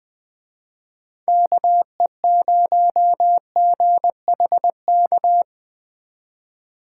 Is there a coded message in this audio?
Yes: Morse code from 1.3 to 5.4 s, heard as a keyed tone, about 20 wpm, 700 Hz. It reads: KE0GHK